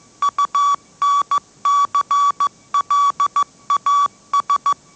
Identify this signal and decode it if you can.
Morse code, heard as a keyed tone, about 14 wpm, 1150 Hz. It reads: UNCLAS
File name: mrx / mrx065 16wpm